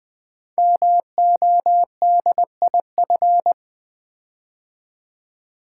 Morse code, heard as a keyed tone, about 20 wpm, 700 Hz. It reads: MODIF
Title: Build A Word - 5 Letters At A Time - 12-Letter Words - 20wpm